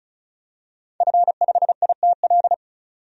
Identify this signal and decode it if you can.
Morse code, heard as a keyed tone, about 34 wpm, 700 Hz. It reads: F5ITL
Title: DXCC-100 Call Signs - 5 Characters - 35wpm